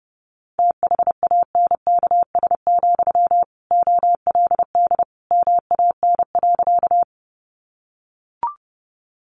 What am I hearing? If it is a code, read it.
Morse code, heard as a keyed tone, about 30 wpm, 700 Hz. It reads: THANKS, OLD MAN.
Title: Sentences from Top 600 Words 30wpm